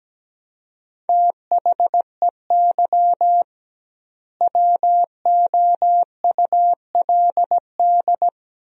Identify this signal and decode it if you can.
Morse code, heard as a keyed tone, about 17 wpm, 700 Hz. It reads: THEY WOULD
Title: 2x Word-Spacing - Sentences from Top 500 Words - 17wpm